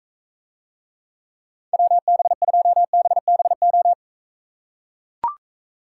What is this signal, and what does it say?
Morse code, heard as a keyed tone, about 41 wpm, 700 Hz. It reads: WB2BBO